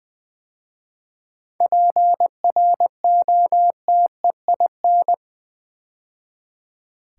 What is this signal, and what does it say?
Morse code, heard as a keyed tone, about 20 wpm, 700 Hz. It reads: PROTEIN
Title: 7 Letter Words - First 3 Letters Given - 20wpm